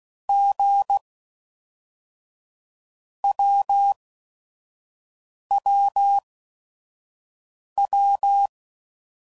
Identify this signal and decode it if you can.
Morse code, heard as a keyed tone, about 16 wpm, 800 Hz. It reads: G W W W
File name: Basic Letter Groups By Blocks